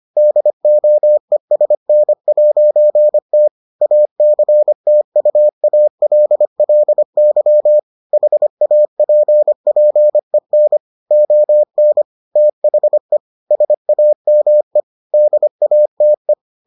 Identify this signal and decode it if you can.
Morse code, heard as a keyed tone, about 25 wpm, 600 Hz. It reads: DOESN'T ACTUALLY HAPPEN ON THE SAME DATE